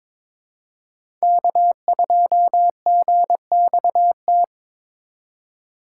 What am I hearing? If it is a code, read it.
Morse code, heard as a keyed tone, about 22 wpm, 700 Hz. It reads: K2GXT
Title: USA Call Signs - 4x Character Spacing - 22wpm